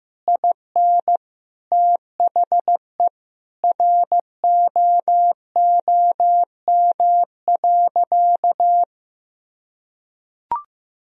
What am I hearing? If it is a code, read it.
Morse code, heard as a keyed tone, about 15 wpm, 700 Hz. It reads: IN THE ROOM.
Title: Sentences from Top 1000 Words - Review Part 3 - 15wpm